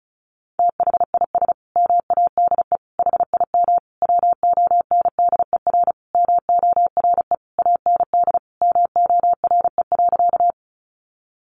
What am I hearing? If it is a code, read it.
Morse code, heard as a keyed tone, about 34 wpm, 700 Hz. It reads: THIS MADE HIM WONDER MORE AND MORE.
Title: Sentences from Top 500 Words 35wpm